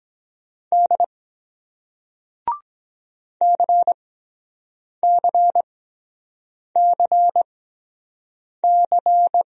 Morse code, heard as a keyed tone, about 23 wpm, spaced as for 7 wpm, 700 Hz. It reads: D CCCC